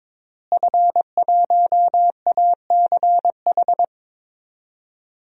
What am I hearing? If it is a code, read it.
Morse code, heard as a keyed tone, about 22 wpm, 700 Hz. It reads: F1ACH
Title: DXCC-100 Call Signs - 5 Characters - Repeated 2x - 22wpm